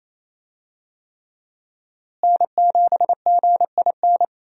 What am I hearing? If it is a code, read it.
Morse code, heard as a keyed tone, about 28 wpm, 700 Hz. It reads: N7GIN
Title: USA Call Signs - 5 Characters - Repeated 3x - 28wpm